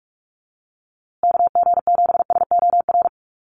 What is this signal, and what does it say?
Morse code, heard as a keyed tone, about 44 wpm, 700 Hz. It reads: KG7SOR
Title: USA Call Signs Repeated 2x 45wpm